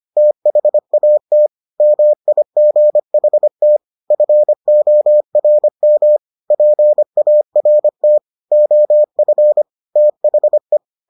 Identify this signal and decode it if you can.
Morse code, heard as a keyed tone, about 25 wpm, 600 Hz. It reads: THAT MIGHT FORM PART OF THE